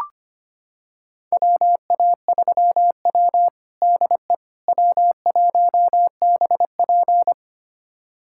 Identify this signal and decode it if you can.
Morse code, heard as a keyed tone, about 25 wpm, 700 Hz. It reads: WA3W DE W1BP
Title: USA Callsign DE Callsign - 4 Characters - 25wpm